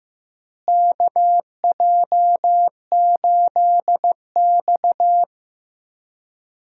Morse code, heard as a keyed tone, about 15 wpm, 700 Hz. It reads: KJ8X
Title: USA Call Signs - 4 Characters - Repeated 3x - 15wpm